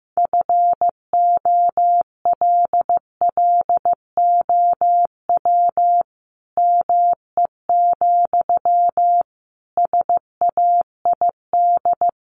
Morse code, heard as a keyed tone, about 14 wpm, 700 Hz. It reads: FOLLOW ME, SAID
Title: Sentences from Top 200 Words 15wpm